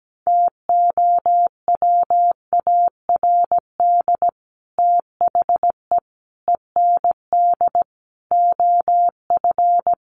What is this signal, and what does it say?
Morse code, heard as a keyed tone, about 17 wpm, 700 Hz. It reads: TOWARD THE END OF
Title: Sets of 5 Words 17wpm